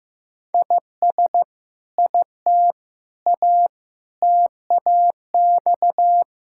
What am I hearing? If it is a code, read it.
Morse code, heard as a keyed tone, about 15 wpm, 700 Hz. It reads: IS IT A TAX
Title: Sentences from Top 1000 Words - Review Part 5 - 15wpm